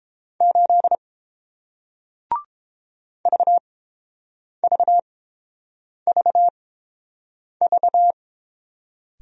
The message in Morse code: 8 VVVV

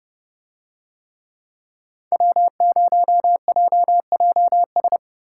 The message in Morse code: W0JJS

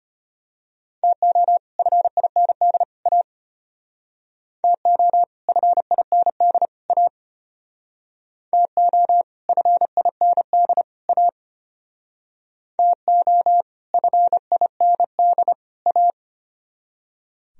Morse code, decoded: TO FIND A TO FIND A TO FIND A TO FIND A

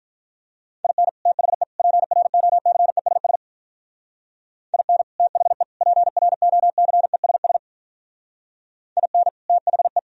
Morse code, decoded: IN THE PROCESS IN THE PROCESS IN THE

